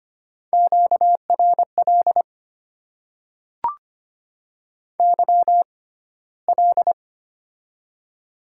Morse code, heard as a keyed tone, about 25 wpm, spaced as for 15 wpm, 700 Hz. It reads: QRL YL
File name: Top 90 QSO Elements - 6x Character Spacing - 25wpm